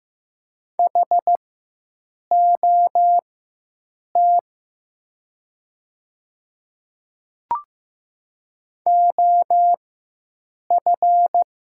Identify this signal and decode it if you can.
Morse code, heard as a keyed tone, about 15 wpm, spaced as for 7 wpm, 700 Hz. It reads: HOT OF
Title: 4x Character Spacing - Top 100 Words - 15wpm